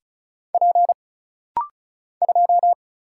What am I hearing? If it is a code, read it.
Morse code, heard as a keyed tone, about 36 wpm, 700 Hz. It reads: P 2
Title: Single Letter-Number - Rapid-Fire 35wpm